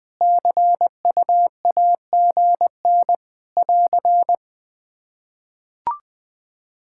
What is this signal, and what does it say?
Morse code, heard as a keyed tone, about 20 wpm, 700 Hz. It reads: CUAGN <AR>
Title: Practice QSO - Exchange 4 - Ending - 20wpm